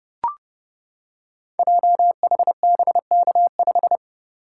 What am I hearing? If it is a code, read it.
Morse code, heard as a keyed tone, about 30 wpm, 700 Hz. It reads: JHBK5
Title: Code Groups - 5 Letter:Numbers - 30wpm